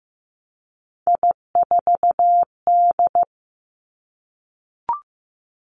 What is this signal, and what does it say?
Morse code, heard as a keyed tone, about 15 wpm, 700 Hz. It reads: I4D
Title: Three Challenging Letter-Number Combinations 15wpm